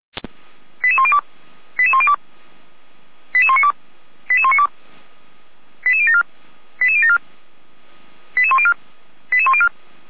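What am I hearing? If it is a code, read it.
Five-tone selective call (ZVEI), heard as quick runs of five tones, tones of about 70 ms. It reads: 88102 88102 88192 88192 88074 88074 88104 88104
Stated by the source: ZVEI five-tone call received by SDR, logged by feuerwehr-orlamuende.org (Feuerwehr Orlamünde).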